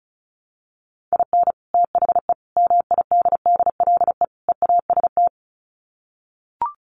Morse code, heard as a keyed tone, about 34 wpm, 700 Hz. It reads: IN THE MIDDLE EAST
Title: Sets of 4 Words 35wpm